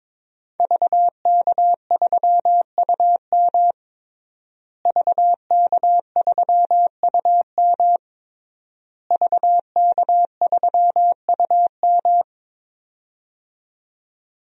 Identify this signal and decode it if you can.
Morse code, heard as a keyed tone, about 22 wpm, 700 Hz. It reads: VK3UM VK3UM VK3UM
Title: DXCC-100 Call Signs - 5 Characters - Repeated 3x - 22wpm